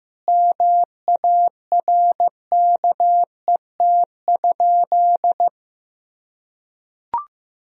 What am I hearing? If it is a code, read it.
Morse code, heard as a keyed tone, about 15 wpm, 700 Hz. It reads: MARKET?
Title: Sentences from Top 1000 Words - Review Part 3 - 15wpm